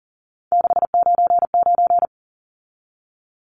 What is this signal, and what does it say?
Morse code, heard as a keyed tone, about 40 wpm, 700 Hz. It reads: 699
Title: Numbers - 3-Digit 40wpm